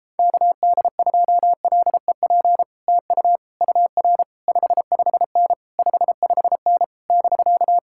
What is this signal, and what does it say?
Morse code, heard as a keyed tone, about 31 wpm, 700 Hz. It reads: KD2LEP TU UR 55N 55N <BK>